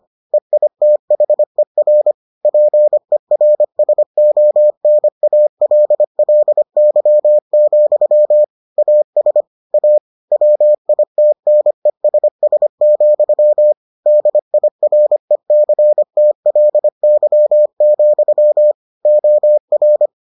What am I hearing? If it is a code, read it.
Morse code, heard as a keyed tone, about 25 wpm, 600 Hz. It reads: EITHER PERSONALLY, AS A WITNESS, DIRECTLY, OR